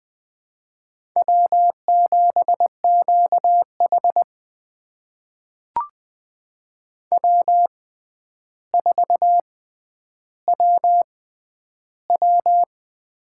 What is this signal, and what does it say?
Morse code, heard as a keyed tone, about 20 wpm, spaced as for 10 wpm, 700 Hz. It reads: W7QH W4WW
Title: USA Call Signs - 4 Characters - 6x Character Spacing - 20wpm